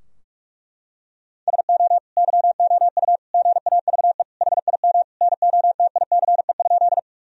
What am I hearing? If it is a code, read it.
Morse code, heard as a keyed tone, about 46 wpm, 700 Hz. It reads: SO YOU GAVE HIM NOTICE?
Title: Sentences from Top 1000 Words - Review Part 2 - 45wpm